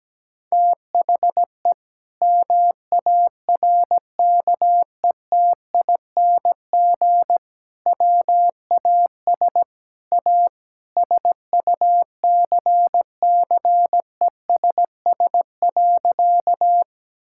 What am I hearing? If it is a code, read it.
Morse code, heard as a keyed tone, about 17 wpm, 700 Hz. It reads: THE MARKETING WAS A SUCCESS.
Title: Sentences from Top 1000 Words - Review Part 5 - 17wpm